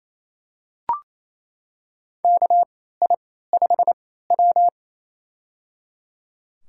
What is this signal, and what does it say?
Morse code, heard as a keyed tone, about 28 wpm, spaced as for 16 wpm, 700 Hz. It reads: KI5W
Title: USA Call Signs - 4 Characters - 3x Character Spacing - 28wpm